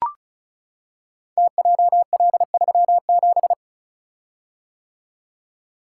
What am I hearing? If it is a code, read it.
Morse code, heard as a keyed tone, about 36 wpm, 690 Hz. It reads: TJL37
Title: Code Groups - 5 Letter:Numbers - 35wpm